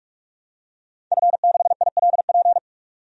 Morse code, heard as a keyed tone, about 46 wpm, 700 Hz. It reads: F6ILP
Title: DXCC-100 Call Signs - 5 Characters - 45wpm